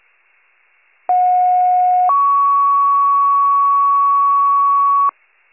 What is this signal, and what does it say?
Two-tone page, heard as a one-second tone, then a three-second tone, about 730 then 1120 Hz. It is heard from 1.1 to 5.1 s.